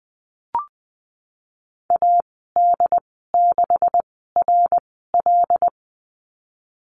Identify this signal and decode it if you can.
Morse code, heard as a keyed tone, about 19 wpm, spaced as for 14 wpm, 700 Hz. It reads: AD6RL
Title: 186 Lesson 30g - Letter b - Call Signs 20wpm